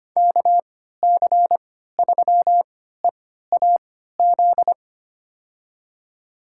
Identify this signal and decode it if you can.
Morse code, heard as a keyed tone, about 25 wpm, spaced as for 14 wpm, 700 Hz. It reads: KC3EAZ
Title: USA Call Signs - 6 Characters - 3x Character Spacing - 25wpm